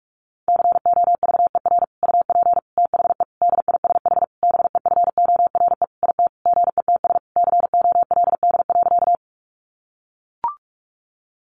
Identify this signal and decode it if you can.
Morse code, heard as a keyed tone, about 44 wpm, 700 Hz. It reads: COVER UP THE DISH BEFORE IT GETS COLD.
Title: Sentences from Top 1300 Words 45wpm